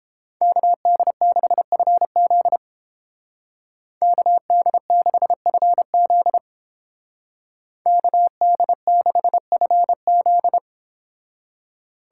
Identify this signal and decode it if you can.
Morse code, heard as a keyed tone, about 30 wpm, 700 Hz. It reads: KD6FZ KD6FZ KD6FZ